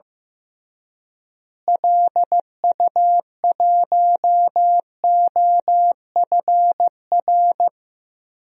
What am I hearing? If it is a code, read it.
Morse code, heard as a keyed tone, about 15 wpm, 700 Hz. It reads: LU1OFR
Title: DXCC-100 Call Signs - 6 Characters - Repeated 3x - 15wpm